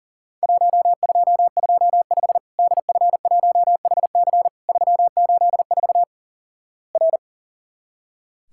Morse code, heard as a keyed tone, about 44 wpm, 700 Hz. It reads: 1225 DF1HC 384 R
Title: Worked All Europe DX Contest - QTC Exchange - 40wpm